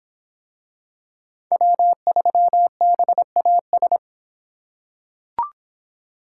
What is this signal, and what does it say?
Morse code, heard as a keyed tone, about 26 wpm, 700 Hz. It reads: W3BAS